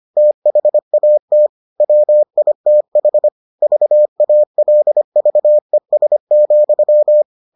Morse code, heard as a keyed tone, about 25 wpm, 600 Hz. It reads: THAT WITH VALVES,